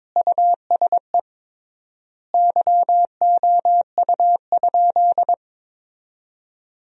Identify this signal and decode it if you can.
Morse code, heard as a keyed tone, about 22 wpm, 700 Hz. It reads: USE YOU?
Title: 3x Word-Spacing - Sentences from Top 200 Words - 22wpm